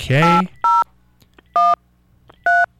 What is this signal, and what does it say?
DTMF, heard as telephone touch tones, tones of about 210 ms, with speech. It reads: 8013